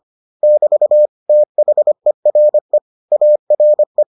Morse code, heard as a keyed tone, about 25 wpm, 600 Hz. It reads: <BT> THERE ARE